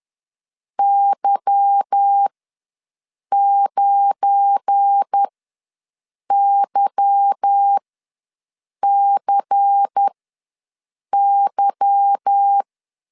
Morse code, heard as a keyed tone, about 10 wpm, spaced as for 6 wpm, 800 Hz. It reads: Y9YCY
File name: LEZIONI / 8 YJ45 / 8 Y ws 10wpm 3952